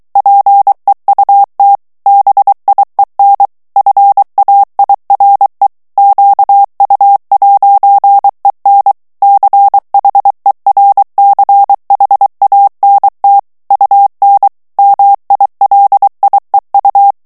Morse code, heard as a keyed tone, about 23 wpm, 800 Hz. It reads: PEUT BIEN FAIRE QU'EN CHERCHANT UN MILIEU
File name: FRANCAIS TEXTE CLAIR 2 VITESSE 22 MOTS MINUTES